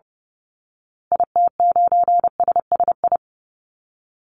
Morse code, heard as a keyed tone, about 30 wpm, 700 Hz. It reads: IT9SSI
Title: Call Signs Repeated 3x - 30wpm